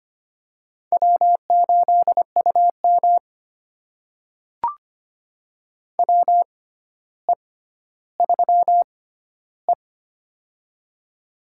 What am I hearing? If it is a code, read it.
Morse code, heard as a keyed tone, about 25 wpm, spaced as for 13 wpm, 700 Hz. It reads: W8UM WE3E
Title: USA Call Signs - 4 Characters - 6x Character Spacing - 25wpm